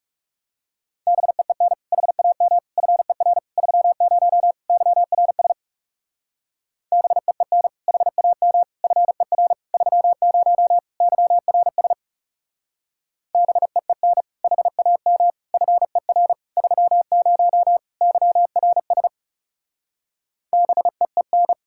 Morse code, heard as a keyed tone, about 37 wpm, 700 Hz. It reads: BEEN HAM FER 30 YRS BEEN HAM FER 30 YRS BEEN HAM FER 30 YRS BEEN
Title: Speed-Racing Common QSO Phrases - 30wpm